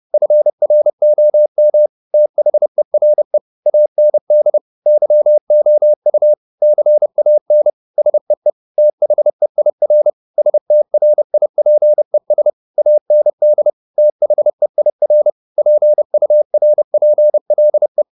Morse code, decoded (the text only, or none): FROM THERE AND YOU CAN SEE THEIR STRIPES AND THEIR PURPLE